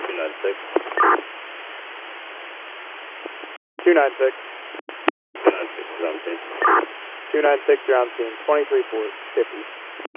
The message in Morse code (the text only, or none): none